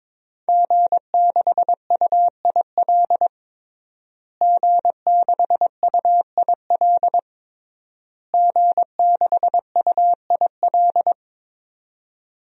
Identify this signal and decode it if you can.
Morse code, heard as a keyed tone, about 22 wpm, 700 Hz. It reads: G6UIL G6UIL G6UIL
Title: DXCC-100 Call Signs - 5 Characters - Repeated 3x - 22wpm